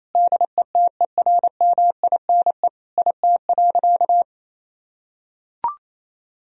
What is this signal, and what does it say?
Morse code, heard as a keyed tone, about 28 wpm, 700 Hz. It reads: DETERMINE IT.
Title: Sentences from Top 1000 Words - Review Part 4 - 28wpm